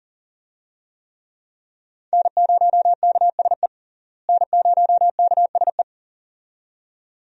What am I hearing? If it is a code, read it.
Morse code, heard as a keyed tone, about 40 wpm, 700 Hz. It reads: N0KSE N0KSE
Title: USA Call Signs - 5 Characters - Repeated 2x - 40wpm